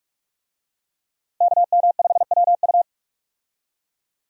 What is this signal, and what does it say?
Morse code, heard as a keyed tone, about 44 wpm, 700 Hz. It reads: KM5WU
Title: USA Call Signs - 5 Characters - Repeated 2x - 45wpm